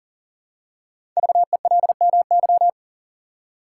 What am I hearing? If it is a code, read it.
Morse code, heard as a keyed tone, about 39 wpm, 700 Hz. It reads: VELMY
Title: Code Groups - 5 Letters - 40wpm